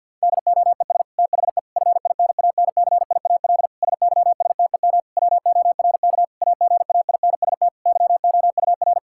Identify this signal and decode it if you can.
Morse code, heard as a keyed tone, about 52 wpm, 690 Hz. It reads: DOES THE FINANCIAL SYSTEM WORK AGAINST YOUR
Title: Sentences from Top 1000 Words - Review Part 5 - 50wpm